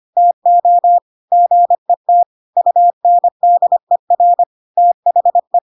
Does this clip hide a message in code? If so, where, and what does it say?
Yes: Morse code from 0.2 to 5.6 s, heard as a keyed tone, about 25 wpm, 700 Hz. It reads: TO GET UNDER THE